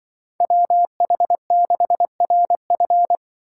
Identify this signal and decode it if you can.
Morse code, heard as a keyed tone, about 23 wpm, 700 Hz. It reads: WH6RF
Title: Speed Racing - USA 2x2 Call Signs 20wpm